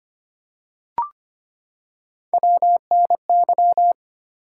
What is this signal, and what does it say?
Morse code, heard as a keyed tone, about 25 wpm, 700 Hz. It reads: WNY
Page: morse-code-ninja-practice-25wpm